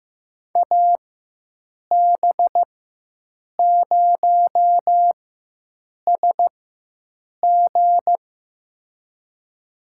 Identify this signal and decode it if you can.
Morse code, heard as a keyed tone, about 15 wpm, spaced as for 7 wpm, 700 Hz. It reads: AB0SG
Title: USA Call Signs - 5 Characters - 4x Character Spacing - 15wpm